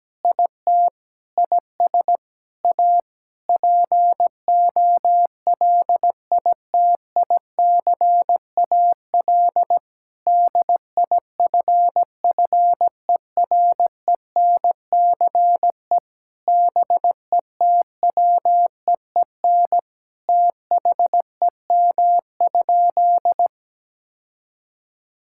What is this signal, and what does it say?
Morse code, heard as a keyed tone, about 17 wpm, 700 Hz. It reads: IT IS A POLITICAL DIFFERENCE BETWEEN THEM?
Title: Sentences from Top 1000 Words - Review Part 5 - 17wpm